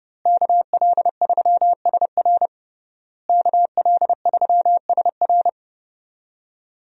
Morse code, heard as a keyed tone, about 30 wpm, 700 Hz. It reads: KL3SR KL3SR
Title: USA Call Signs - 5 Characters - Repeated 3x - 30wpm